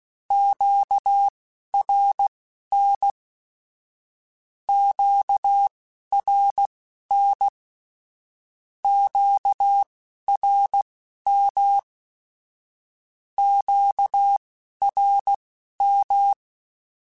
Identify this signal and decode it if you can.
Morse code, heard as a keyed tone, about 16 wpm, spaced as for 11 wpm, 800 Hz. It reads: QRN QRN QRM QRM
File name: (spaced) / Q-Calls Spaced-mp3s / Q Calls sp Once @15